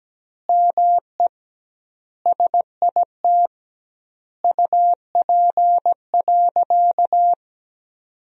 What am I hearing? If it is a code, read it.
Morse code, heard as a keyed tone, about 17 wpm, 700 Hz. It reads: ME SIT UP.
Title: 2x Word-Spacing - Sentences from Top 500 Words - 17wpm